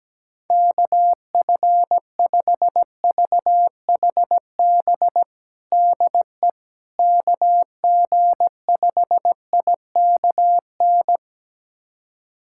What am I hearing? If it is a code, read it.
Morse code, heard as a keyed tone, about 17 wpm, 700 Hz. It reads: KF5VHB DE KG5IKN
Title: USA Callsign DE Callsign - 6 Characters - 17wpm